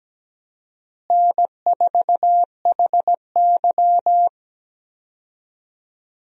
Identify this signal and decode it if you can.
Morse code, heard as a keyed tone, about 17 wpm, 700 Hz. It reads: N4HY
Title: USA Call Signs - 4 Characters - Repeated 2x - 17wpm